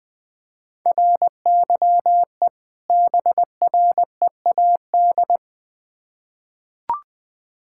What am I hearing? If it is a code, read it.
Morse code, heard as a keyed tone, about 21 wpm, 700 Hz. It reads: RYE BREAD
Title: Difficult Words with Hints - 20wpm